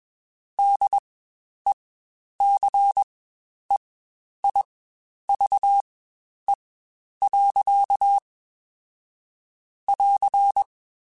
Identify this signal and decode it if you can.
Morse code, heard as a keyed tone, about 21 wpm, spaced as for 10 wpm, 800 Hz. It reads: DECEIVE. <AR>
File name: Quotes Spaced / Quotes Spaced-mp3s / Deceive sp @20